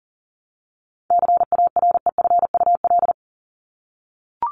Morse code, heard as a keyed tone, about 40 wpm, 700 Hz. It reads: CAREFUL
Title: LY Suffix - Top 100 Words with Base Words 40wpm